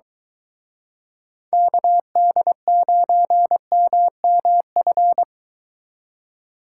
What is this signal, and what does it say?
Morse code, heard as a keyed tone, about 24 wpm, 700 Hz. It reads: KD9MMF